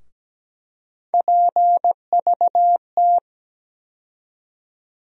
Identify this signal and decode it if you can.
Morse code, heard as a keyed tone, about 17 wpm, 700 Hz. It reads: PVT
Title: USA Call Sign Postfixes - 3 Letters 17wpm